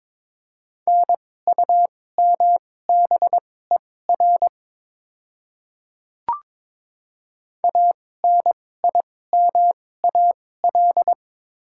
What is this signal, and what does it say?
Morse code, heard as a keyed tone, about 21 wpm, spaced as for 16 wpm, 700 Hz. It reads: NUMBER ANIMAL
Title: 2x Character Spacing - Top 300 Words - 22wpm